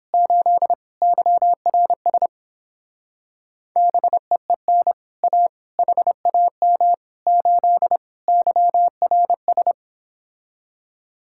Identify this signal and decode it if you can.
Morse code, heard as a keyed tone, about 29 wpm, 700 Hz. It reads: 8 YRS BEEN A HAM 8 YRS